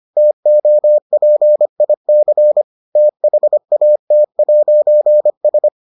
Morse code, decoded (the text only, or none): TOPIC THAT'S